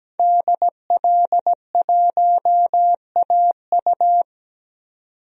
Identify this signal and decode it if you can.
Morse code, heard as a keyed tone, about 17 wpm, 700 Hz. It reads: DL1AU
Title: DXCC-100 Call Signs - 5 Characters - 17wpm